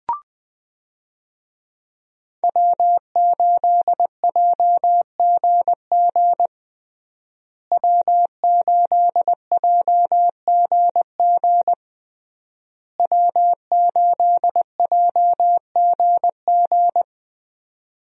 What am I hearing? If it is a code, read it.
Morse code, heard as a keyed tone, about 21 wpm, 700 Hz. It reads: W8JGG W8JGG W8JGG
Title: USA Call Signs - 5 Characters - Repeated 3x - 20wpm